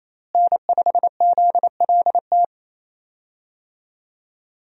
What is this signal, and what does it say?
Morse code, heard as a keyed tone, about 28 wpm, 700 Hz. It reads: N5ZLT
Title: USA Call Signs - 5 Characters - Repeated 2x - 28wpm